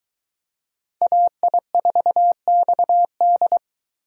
Morse code, heard as a keyed tone, about 23 wpm, 700 Hz. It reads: AI4XD